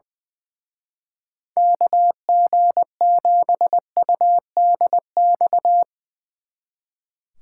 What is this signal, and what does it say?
Morse code, heard as a keyed tone, about 21 wpm, 700 Hz. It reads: KG7UDX
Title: Speed Racing - USA 2x3 Call Signs 20wpm